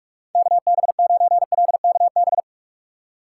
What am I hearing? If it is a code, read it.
Morse code, heard as a keyed tone, about 44 wpm, 700 Hz. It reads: KB9LKB
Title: USA Call Signs - 6 Characters - 6x Character Spacing - 45wpm